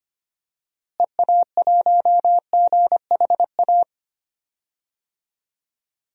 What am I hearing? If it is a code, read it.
Morse code, heard as a keyed tone, about 25 wpm, 700 Hz. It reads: EA1GHA
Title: DXCC-100 Call Signs - 6 Characters - Repeated 3x - 25wpm